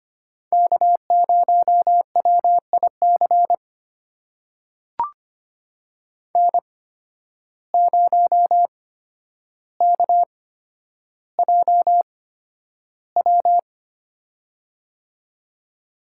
Morse code, heard as a keyed tone, about 25 wpm, spaced as for 11 wpm, 700 Hz. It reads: K0WIC N0KJW
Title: USA Call Signs - 5 Characters - 8x Character Spacing - 25wpm